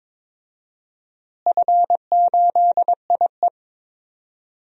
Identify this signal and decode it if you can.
Morse code, heard as a keyed tone, about 22 wpm, 700 Hz. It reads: F8IE
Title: DXCC-100 Call Signs - 4 Characters - Repeated 2x - 22wpm